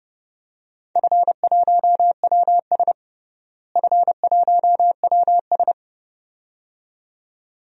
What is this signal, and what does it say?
Morse code, heard as a keyed tone, about 30 wpm, 700 Hz. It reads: F1WS F1WS
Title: DXCC-100 Call Signs - 4 Characters - Repeated 2x - 30wpm